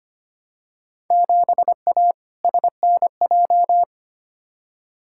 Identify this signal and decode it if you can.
Morse code, heard as a keyed tone, about 25 wpm, 700 Hz. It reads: 7A SNJ
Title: Field Day Exchanges Encore - 25wpm